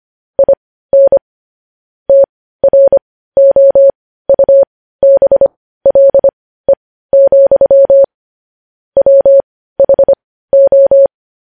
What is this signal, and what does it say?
Morse code, heard as a keyed tone, about 25 wpm, spaced as for 15 wpm, 550 Hz. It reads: IN TROUBLE, WHO
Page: news-headlines-in-morse-code-at-15-wpm